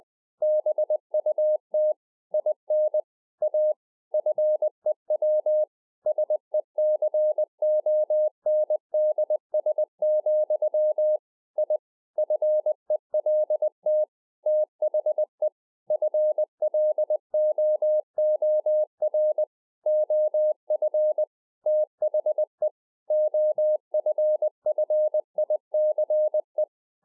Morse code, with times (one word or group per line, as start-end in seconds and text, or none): 0.4-1.9 BUT
2.3-3.0 IN
3.4-3.7 A
4.1-5.6 FEW
6.1-11.2 SECONDS,
11.6-11.8 I
12.2-14.0 FELT
14.5-15.5 THE
15.9-19.4 FLOOR
19.9-21.2 OF
21.7-22.7 THE
23.1-26.6 OFFICE